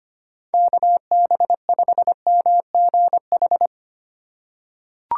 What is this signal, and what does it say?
Morse code, heard as a keyed tone, about 25 wpm, 700 Hz. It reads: KB5MGH
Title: USA Call Signs - 4x Character Spacing - 25wpm